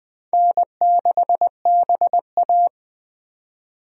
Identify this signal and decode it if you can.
Morse code, heard as a keyed tone, about 20 wpm, 700 Hz. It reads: N6BA
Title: USA Call Signs - 4 Characters - 20wpm